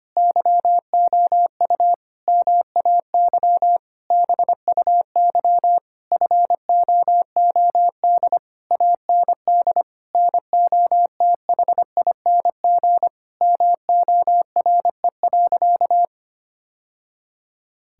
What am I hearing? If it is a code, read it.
Morse code, heard as a keyed tone, about 25 wpm, 700 Hz. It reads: YOU MAY BUY FOOD AND NOTHING MORE.
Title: Sentences from Top 1000 Words - Review Part 5 - 25wpm